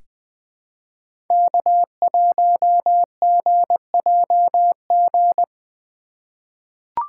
Morse code, heard as a keyed tone, about 20 wpm, 700 Hz. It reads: K1GJG
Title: USA Call Signs - 5 Characters - 3x Character Spacing - 20wpm